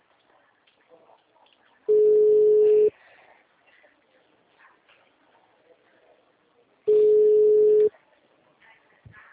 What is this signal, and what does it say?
Ringback tone (Europe, 425 Hz).